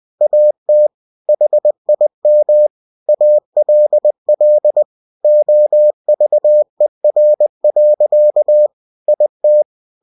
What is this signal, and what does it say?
Morse code, heard as a keyed tone, about 20 wpm, 600 Hz. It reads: AT HIM ALL OVER. IT
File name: cw-books / winnie-the-pooh / winnie-20wpm-page-0015